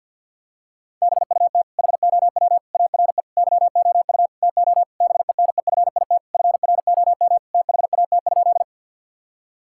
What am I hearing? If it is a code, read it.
Morse code, heard as a keyed tone, about 50 wpm, 700 Hz. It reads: BUT HOW ARE YOU TO BENEFIT FROM THAT?
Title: Sentences from Top 1000 Words - Review Part 6 - 50wpm